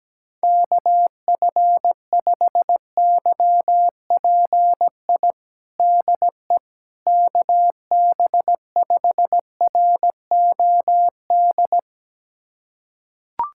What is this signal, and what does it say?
Morse code, heard as a keyed tone, about 17 wpm, 700 Hz. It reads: KF5YPI DE KB5ROD
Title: USA Callsign DE Callsign 17wpm